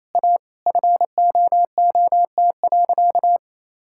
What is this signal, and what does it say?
Morse code, heard as a keyed tone, about 28 wpm, 700 Hz. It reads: A FOOT.
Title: Sentences from Top 1600 Words 28wpm